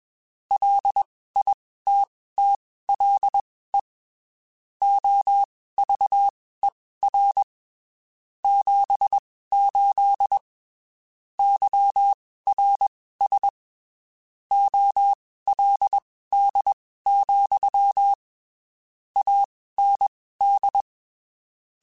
Morse code, heard as a keyed tone, about 21 wpm, spaced as for 15 wpm, 800 Hz. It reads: LITTLE OVER 78 YRS OLD, AND